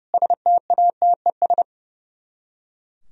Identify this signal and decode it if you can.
Morse code, heard as a keyed tone, about 38 wpm, 700 Hz. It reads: STATES